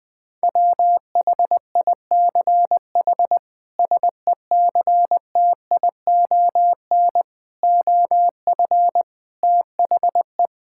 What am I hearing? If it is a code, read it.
Morse code, heard as a keyed tone, about 20 wpm, 700 Hz. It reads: WHICH SECTION OF THE
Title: Sentences from Top 1000 Words - Review Part 5 - 20wpm